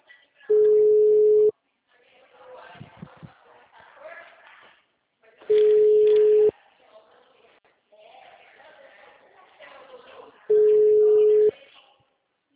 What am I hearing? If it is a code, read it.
Ringback tone (Europe, 425 Hz), with speech.
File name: out- -2025- - - .52938